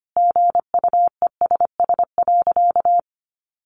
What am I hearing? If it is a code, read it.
Morse code, heard as a keyed tone, about 25 wpm, 700 Hz. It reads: GUESS.
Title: Sentences from Top 1500 Words 25wpm